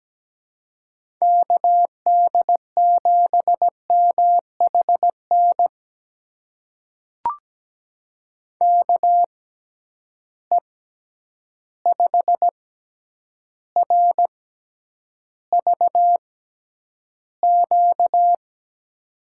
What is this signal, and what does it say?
Morse code, heard as a keyed tone, about 17 wpm, spaced as for 9 wpm, 700 Hz. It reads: KD7MHN KE5RVQ
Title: USA Call Signs - 6 Characters - 6x Character Spacing - 17wpm